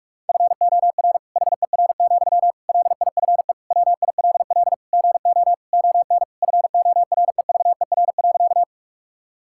Morse code, heard as a keyed tone, about 41 wpm, 700 Hz. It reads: FOR HER, LIFE WILL GO ON FOREVER.
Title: Sentences from Top 1600 Words 45wpm